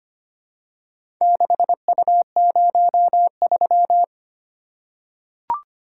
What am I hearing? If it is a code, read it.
Morse code, heard as a keyed tone, about 25 wpm, 700 Hz. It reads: 6U03
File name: practice / 25wpm / Four Letter-Number Combinations - 25wpm